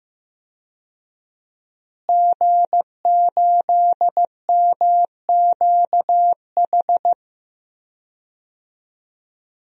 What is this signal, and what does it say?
Morse code, heard as a keyed tone, about 15 wpm, 700 Hz. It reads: G8MQH
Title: DXCC-100 Call Signs - 5 Characters - Repeated 3x - 15wpm